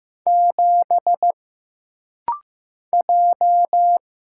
Morse code, heard as a keyed tone, about 15 wpm, 700 Hz. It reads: 7 J
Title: Single Letter-Number Alt - Rapid-Fire 15wpm